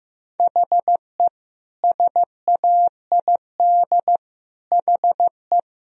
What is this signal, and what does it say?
Morse code, heard as a keyed tone, about 15 wpm, 700 Hz. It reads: HE SAID HE
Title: Sentences from Top 1000 Words - Review Part 3 - 15wpm